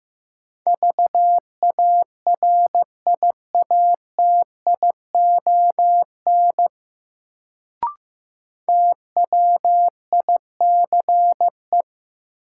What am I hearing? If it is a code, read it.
Morse code, heard as a keyed tone, about 15 wpm, 700 Hz. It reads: VARIATION TWICE
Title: Sentences from Top 1600 Words - New Word Drill - 15wpm